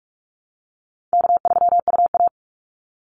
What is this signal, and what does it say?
Morse code, heard as a keyed tone, about 44 wpm, 700 Hz. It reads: K3UA
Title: Call Signs - 45wpm